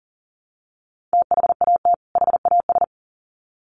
Morse code, heard as a keyed tone, about 40 wpm, 700 Hz. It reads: THAT HAS
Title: Sets of 2 Words 40wpm